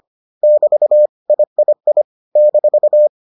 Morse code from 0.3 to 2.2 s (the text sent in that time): <BT> III